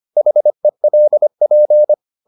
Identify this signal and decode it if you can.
Morse code, heard as a keyed tone, about 25 wpm, 600 Hz. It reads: HELP
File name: the old man and the sea part 1 25wpm